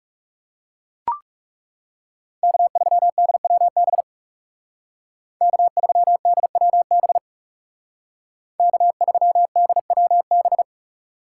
Morse code, heard as a keyed tone, about 36 wpm, 700 Hz. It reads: K3DWB K3DWB K3DWB
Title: Speed Racing - USA 1x3 Call Signs 30wpm